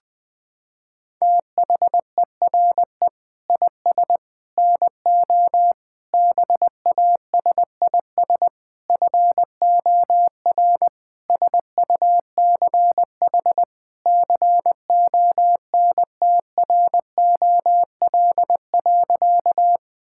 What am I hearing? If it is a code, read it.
Morse code, heard as a keyed tone, about 20 wpm, 700 Hz. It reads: THERE IS NO BASIS FOR SUCH CONTROL.
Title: Sentences from Top 1000 Words - Review Part 5 - 20wpm